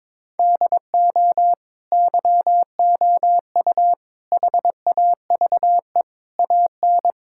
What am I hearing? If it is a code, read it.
Morse code, heard as a keyed tone, about 22 wpm, 700 Hz. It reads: DO YOU HAVE AN
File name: Sets of 9 Words - 22wpm